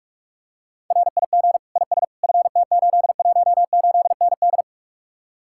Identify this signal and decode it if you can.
Morse code, heard as a keyed tone, about 44 wpm, 700 Hz. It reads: RIG IS FT818ND